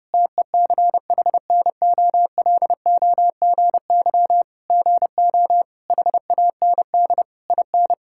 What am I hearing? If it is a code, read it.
Morse code, heard as a keyed tone, about 30 wpm, 700 Hz. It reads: TECHNOLOGY GO HAND IN
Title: Sentences from Top 1000 Words - Review Part 5 - 30wpm